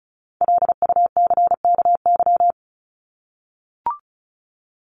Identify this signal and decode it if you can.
Morse code, heard as a keyed tone, about 34 wpm, 700 Hz. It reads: LUCKY
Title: Top 1751-2000 Words 35wpm